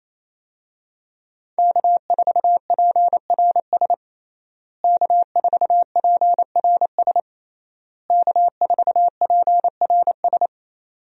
Morse code, heard as a keyed tone, about 28 wpm, 700 Hz. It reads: K4PRS K4PRS K4PRS
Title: USA Call Signs - 5 Characters - Repeated 3x - 28wpm